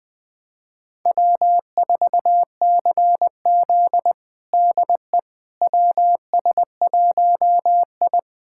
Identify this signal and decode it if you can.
Morse code, heard as a keyed tone, about 20 wpm, 700 Hz. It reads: W4CZ DE WS1I
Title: USA Callsign DE Callsign - 4 Characters - 20wpm